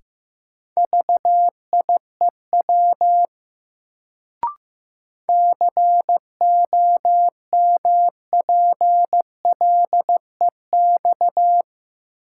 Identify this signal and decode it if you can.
Morse code, heard as a keyed tone, about 15 wpm, 700 Hz. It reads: VIEW COMPLEX
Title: Sentences from Top 900 Words - New Word Drill - 15wpm